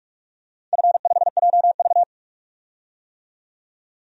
Morse code, heard as a keyed tone, about 42 wpm, 700 Hz. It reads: F5JV
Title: DXCC-100 Call Signs - 4 Characters - 45wpm